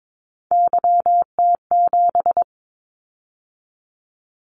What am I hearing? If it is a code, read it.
Morse code, heard as a keyed tone, about 22 wpm, 700 Hz. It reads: YT7
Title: Callsign Prefixes for Top 25 DXCC - 22wpm